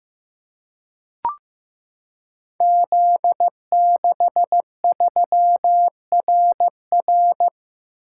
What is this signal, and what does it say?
Morse code, heard as a keyed tone, about 15 wpm, 700 Hz. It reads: Z63RR
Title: Code Groups - 5 Letter:Numbers - 15wpm